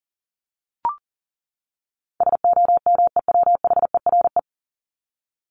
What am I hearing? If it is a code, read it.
Morse code, heard as a keyed tone, about 40 wpm, 700 Hz. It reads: SOMEWHERE
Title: Top 1251-1500 Words 40wpm